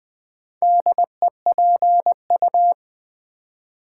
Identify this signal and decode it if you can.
Morse code, heard as a keyed tone, about 20 wpm, 700 Hz. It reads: DEPU
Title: Build A Word - 2 Letters At A Time - 6-Letter Words - 20wpm